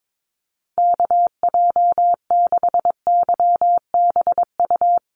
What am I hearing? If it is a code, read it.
Morse code, heard as a keyed tone, about 22 wpm, 700 Hz. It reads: KJ6YBU
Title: USA Call Signs Repeated 2x 22wpm